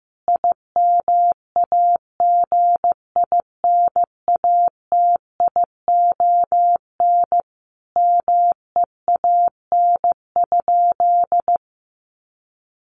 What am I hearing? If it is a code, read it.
Morse code, heard as a keyed tone, about 15 wpm, 700 Hz. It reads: IMAGINATION MEAN?
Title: Sentences from Top 1300 Words 15wpm